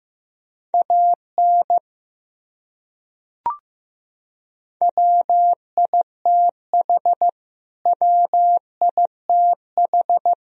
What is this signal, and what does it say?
Morse code, heard as a keyed tone, about 16 wpm, 700 Hz. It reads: AN WITH WITH
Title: Top 100 Words Repeated 15wpm